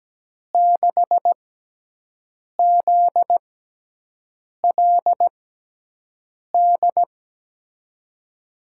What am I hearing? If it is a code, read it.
Morse code, heard as a keyed tone, about 17 wpm, spaced as for 6 wpm, 700 Hz. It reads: 6ZLD
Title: USA Call Signs - 6 Characters - 6x Character Spacing - 17wpm